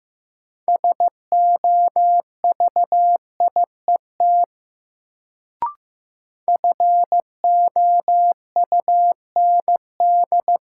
Morse code, heard as a keyed tone, about 15 wpm, 700 Hz. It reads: SOVIET FOUND
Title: Sets of 5 Words Encore - New Word Drill - 15wpm